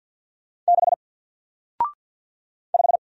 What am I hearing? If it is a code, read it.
Morse code, heard as a keyed tone, about 49 wpm, 700 Hz. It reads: 6 5
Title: Single Character (Letter:Number:Punctuation:Prosign) Alt 50wpm